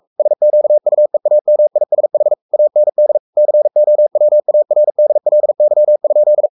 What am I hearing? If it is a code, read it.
Morse code, heard as a keyed tone, about 44 wpm, 600 Hz. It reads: SQUEAMISH AND COWARDLY?
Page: Romanen Never af Ken Follet på engelsk.